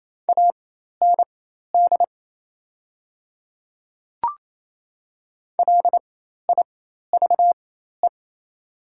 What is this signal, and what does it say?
Morse code, heard as a keyed tone, about 28 wpm, spaced as for 13 wpm, 700 Hz. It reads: AND LIVE
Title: 4x Character Spacing - Top 300 Words - 28wpm